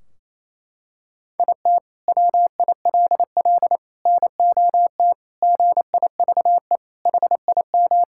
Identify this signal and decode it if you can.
Morse code, heard as a keyed tone, about 28 wpm, 700 Hz. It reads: IT WILL NOT GIVE HIM
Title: Sentences from Top 1600 Words 28wpm